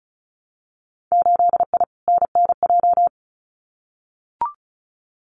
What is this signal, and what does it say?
Morse code, heard as a keyed tone, about 36 wpm, 700 Hz. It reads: 8I NNJ
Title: Winter Field Day Exchanges Encore 35wpm